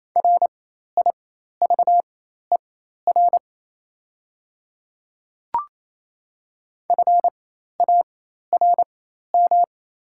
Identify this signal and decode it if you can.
Morse code, heard as a keyed tone, about 27 wpm, spaced as for 13 wpm, 700 Hz. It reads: RIVER FARM
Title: 4x Character Spacing - Top 500 Words - 28wpm